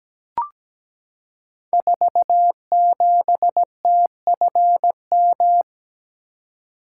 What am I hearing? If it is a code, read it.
Morse code, heard as a keyed tone, about 17 wpm, 700 Hz. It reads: 47TFM
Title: Code Groups - 5 Letter:Numbers - 17wpm